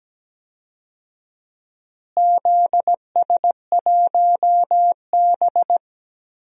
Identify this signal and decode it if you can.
Morse code, heard as a keyed tone, about 17 wpm, 700 Hz. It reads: ZS1B
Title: DXCC-100 Call Signs - 4 Characters - Repeated 2x - 17wpm